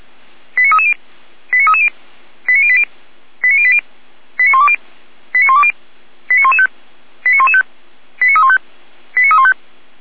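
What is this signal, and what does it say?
Five-tone selective call (ZVEI), heard as quick runs of five tones, tones of about 70 ms. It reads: 89330 89330 89080 89080 89120 89120 89115 89115 89315 89315
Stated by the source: ZVEI five-tone call received on an SDR.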